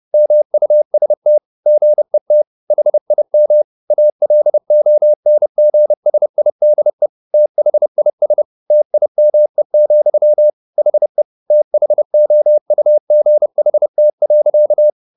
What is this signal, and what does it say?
Morse code, heard as a keyed tone, about 30 wpm, 600 Hz. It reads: MUST GET HIM ALONGSIDE THIS TIME, HE THOUGHT.